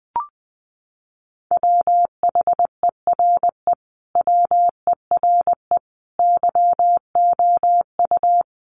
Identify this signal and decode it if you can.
Morse code, heard as a keyed tone, about 19 wpm, 700 Hz. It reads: WHERE WERE YOU
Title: Sentences from Top 1000 Words 20wpm